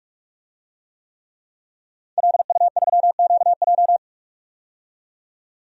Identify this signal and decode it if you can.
Morse code, heard as a keyed tone, about 46 wpm, 700 Hz. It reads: LU3QJ
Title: DXCC-100 Call Signs - 5 Characters - 45wpm